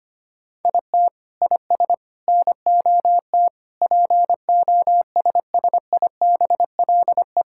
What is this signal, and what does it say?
Morse code, heard as a keyed tone, about 25 wpm, 700 Hz. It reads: IT IS NOT POSSIBLE
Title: Sentences from Top 1000 Words - Review Part 5 - 25wpm